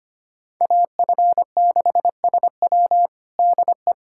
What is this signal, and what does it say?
Morse code, heard as a keyed tone, about 25 wpm, 700 Hz. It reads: AF6SW DE
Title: USA Callsign DE Callsign - 5 Characters - 25wpm